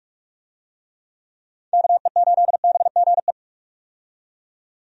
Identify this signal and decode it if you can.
Morse code, heard as a keyed tone, about 46 wpm, 700 Hz. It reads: KE8BGE
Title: USA Call Signs - 6 Characters - 45wpm